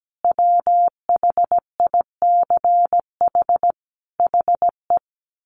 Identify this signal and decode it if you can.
Morse code, heard as a keyed tone, about 17 wpm, 700 Hz. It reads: WHICH HE
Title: Sentences from Top 1500 Words 17wpm